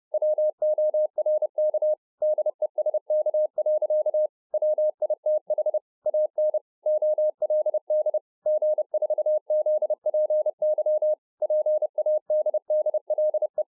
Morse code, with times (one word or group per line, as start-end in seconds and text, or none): 0.1-1.9 WORK
2.2-4.3 DESK.
4.5-5.8 WITH
6.1-6.6 AN
6.9-8.2 OLD
8.5-11.1 G4ZPY
11.4-13.6 PADDLE